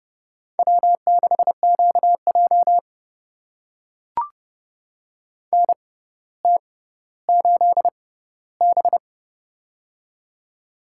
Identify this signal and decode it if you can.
Morse code, heard as a keyed tone, about 29 wpm, spaced as for 15 wpm, 700 Hz. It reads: W6QJ NT8B
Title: USA Call Signs - 4 Characters - 6x Character Spacing - 30wpm